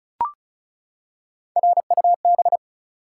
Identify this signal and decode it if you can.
Morse code, heard as a keyed tone, about 34 wpm, 700 Hz. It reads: RUB
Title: Three Letter Words - 35wpm